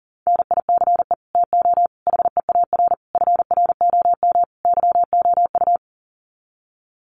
Morse code, heard as a keyed tone, about 40 wpm, 700 Hz. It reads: NICE TO HEAR FROM YOU
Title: Common QSO Phrases 40wpm